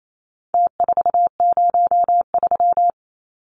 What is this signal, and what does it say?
Morse code, heard as a keyed tone, about 28 wpm, 700 Hz. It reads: T403
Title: Numbers - Contest Style - 4-Digit 28wpm